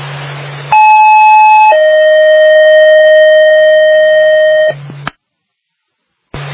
Two-tone page, heard as a one-second tone, then a three-second tone, about 860 then 620 Hz. It is heard from 0.7 to 4.7 s.